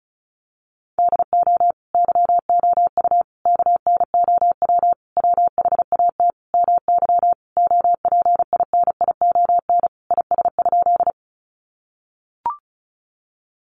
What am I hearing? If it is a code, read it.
Morse code, heard as a keyed tone, about 34 wpm, 700 Hz. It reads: DO YOU KNOW WHAT MY OPINION IS?
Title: Sentences from Top 1200 Words 35wpm